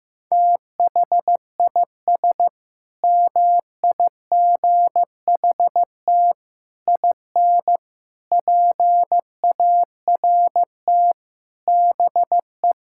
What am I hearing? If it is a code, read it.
Morse code, heard as a keyed tone, about 15 wpm, 700 Hz. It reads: THIS MIGHT IN PART BE
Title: Sentences from Top 1000 Words - Review Part 5 - 15wpm